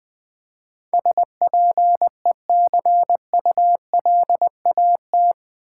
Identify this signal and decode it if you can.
Morse code, heard as a keyed tone, about 20 wpm, 700 Hz. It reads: SPECULAT
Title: Build A Word - 4 Letters At A Time - 11-Letter Words - 20wpm